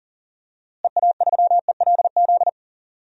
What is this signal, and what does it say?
Morse code, heard as a keyed tone, about 40 wpm, 700 Hz. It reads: EA3ELZ
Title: DXCC-100 Call Signs - 6 Characters - Repeated 2x - 40wpm